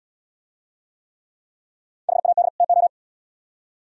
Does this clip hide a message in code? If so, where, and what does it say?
Yes: Morse code from 2.1 to 2.9 s, heard as a keyed tone, about 78 wpm, 700 Hz. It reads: HAD TO